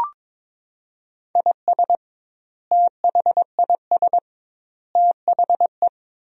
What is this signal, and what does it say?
Morse code, heard as a keyed tone, about 22 wpm, 700 Hz. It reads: IS THIS THE
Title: 2x Word-Spacing - Sentences from Top 200 Words - 22wpm